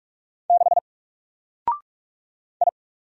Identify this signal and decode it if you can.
Morse code, heard as a keyed tone, about 44 wpm, 700 Hz. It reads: 6 I
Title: Single Character (Letter:Number:Punctuation:Prosign) Alt 45wpm